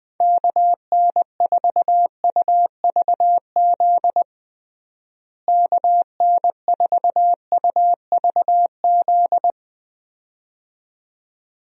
Morse code, heard as a keyed tone, about 20 wpm, 700 Hz. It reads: KN4UVZ KN4UVZ
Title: USA Call Signs - 6 Characters - Repeated 2x - 20wpm